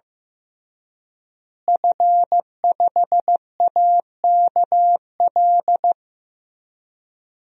Morse code, heard as a keyed tone, about 15 wpm, 700 Hz. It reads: F5AKL
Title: DXCC-100 Call Signs - 5 Characters - Repeated 2x - 15wpm